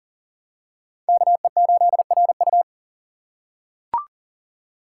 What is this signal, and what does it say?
Morse code, heard as a keyed tone, about 40 wpm, 690 Hz. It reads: KE8RU
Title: K1USN SST QSO Practice - 40wpm